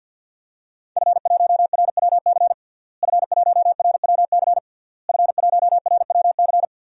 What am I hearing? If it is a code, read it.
Morse code, heard as a keyed tone, about 51 wpm, 700 Hz. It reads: F1RWC F1RWC F1RWC
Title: DXCC-100 Call Signs - 5 Characters - Repeated 3x - 50wpm